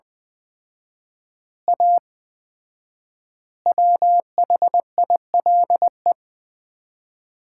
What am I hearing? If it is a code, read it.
Morse code, heard as a keyed tone, about 20 wpm, 700 Hz. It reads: A WHILE
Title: 4x Word-Spacing - Sets of 3 Words - 20wpm